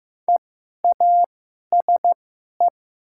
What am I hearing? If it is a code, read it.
Morse code, heard as a keyed tone, about 15 wpm, spaced as for 11 wpm, 700 Hz. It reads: EASE
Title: 2x Character Spacing - Top 300 Words - 15wpm